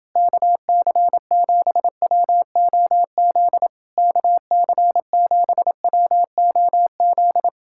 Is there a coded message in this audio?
Yes: Morse code from 0.2 to 7.5 s, heard as a keyed tone, about 29 wpm, 700 Hz. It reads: KC7WOZ KC7WOZ